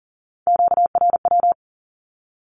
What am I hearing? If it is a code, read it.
Morse code, heard as a keyed tone, about 42 wpm, 700 Hz. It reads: QRW
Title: Q-Codes with Spoken Meaning - 40wpm